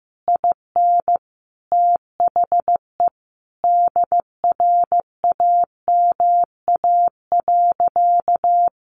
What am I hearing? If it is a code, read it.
Morse code, heard as a keyed tone, about 14 wpm, 700 Hz. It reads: IN THE DRAMA.
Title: Sentences from Top 1500 Words 15wpm